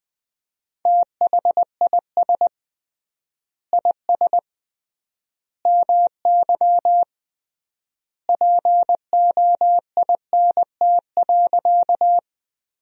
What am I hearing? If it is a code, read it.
Morse code, heard as a keyed tone, about 20 wpm, 700 Hz. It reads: THIS IS MY POINT.
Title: 3x Word-Spacing - Sentences from Top 200 Words - 20wpm